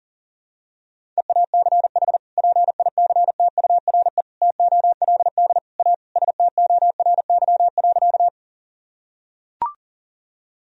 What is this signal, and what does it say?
Morse code, heard as a keyed tone, about 40 wpm, 700 Hz. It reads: EACH PICTURE TOLD A STORY.
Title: Sentences from Top 1000 Words - Review Part 1 - 40wpm